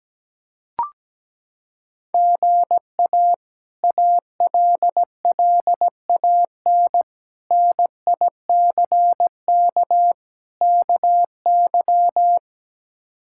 Morse code, heard as a keyed tone, about 17 wpm, 700 Hz. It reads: GA ALLAN NICK KY
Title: K1USN SST QSO Practice - 17wpm